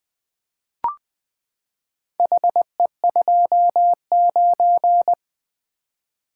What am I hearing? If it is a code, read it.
Morse code, heard as a keyed tone, about 20 wpm, 700 Hz. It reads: HE29
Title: Four Letter-Number Combinations - 20wpm